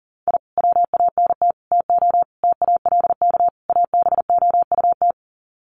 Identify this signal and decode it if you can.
Morse code, heard as a keyed tone, about 40 wpm, 700 Hz. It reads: I WANT TO TALK ABOUT
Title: Sets of 5 Words Encore 40wpm